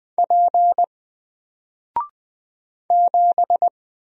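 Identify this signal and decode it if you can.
Morse code, heard as a keyed tone, about 20 wpm, 700 Hz. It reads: P 7
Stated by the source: Single letter and number Morse drill at 20 wpm, from N4LCD.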